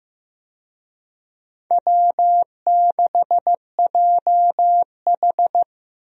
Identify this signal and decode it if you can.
Morse code, heard as a keyed tone, about 16 wpm, 700 Hz. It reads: W6JH
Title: USA Call Signs - 4 Characters - Repeated 3x - 15wpm